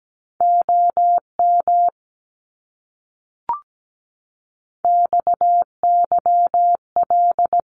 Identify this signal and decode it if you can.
Morse code, heard as a keyed tone, about 17 wpm, 700 Hz. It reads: OM XYL
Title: Abbreviations Repeated 3x - 17wpm